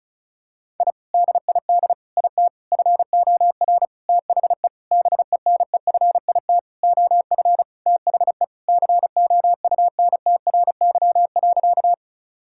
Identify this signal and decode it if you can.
Morse code, heard as a keyed tone, about 36 wpm, 690 Hz. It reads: I DID IT FOR THE BENEFIT OF THE COUNTRY.
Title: Sentences from Top 1000 Words - Review Part 6 - 35wpm